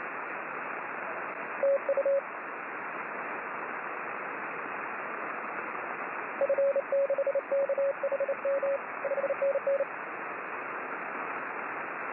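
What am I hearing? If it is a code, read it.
Morse code, heard as a keyed tone, about 29 wpm, 550 Hz, with radio noise. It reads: TU F6KHM 5NN
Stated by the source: Morse signal on the 6 m band.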